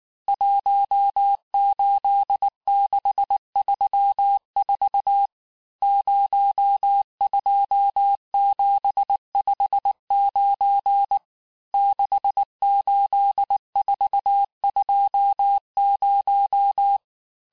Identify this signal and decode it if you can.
Morse code, heard as a keyed tone, about 19 wpm, 800 Hz. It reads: 18634 02759 68420
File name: cours-du-lundi files / cours-1583--texte-chiffre-18-mpm